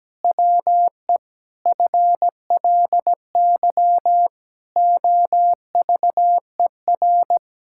Morse code, heard as a keyed tone, about 17 wpm, 700 Hz. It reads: WE FLY OVER
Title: Pangram Challenge v2 - 17wpm